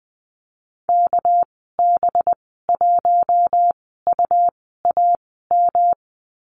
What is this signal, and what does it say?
Morse code, heard as a keyed tone, about 20 wpm, spaced as for 15 wpm, 700 Hz. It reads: KB1UAM
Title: 213 Lesson 34g - Number 8 - Call Signs 20wpm